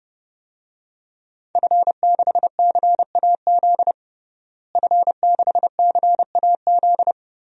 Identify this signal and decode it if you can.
Morse code, heard as a keyed tone, about 32 wpm, 700 Hz. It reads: F6CAZ F6CAZ
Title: DXCC-100 Call Signs - 5 Characters - Repeated 3x - 30wpm